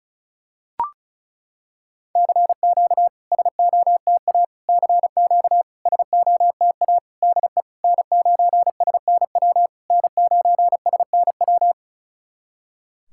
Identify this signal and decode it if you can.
Morse code, heard as a keyed tone, about 36 wpm, 700 Hz. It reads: CQ SOTA CQ SOTA DE N9SNW N9SNW
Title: SOTA QSO Practice 35wpm